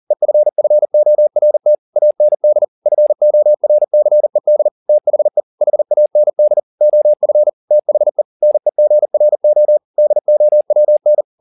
Morse code, heard as a keyed tone, about 40 wpm, 600 Hz. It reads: EFFORT AND FORCED THE HAND OF THE NEGRO DOWN
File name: the old man and the sea 2 40wpm